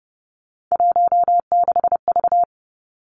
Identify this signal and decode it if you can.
Morse code, heard as a keyed tone, about 30 wpm, 700 Hz. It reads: 16V
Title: Three Challenging Letter-Number Combinations 30wpm